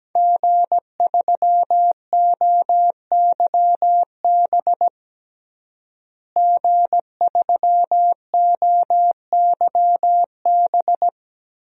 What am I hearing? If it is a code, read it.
Morse code, heard as a keyed tone, about 17 wpm, 700 Hz. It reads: G3OYB G3OYB
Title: DXCC-100 Call Signs - 5 Characters - Repeated 2x - 17wpm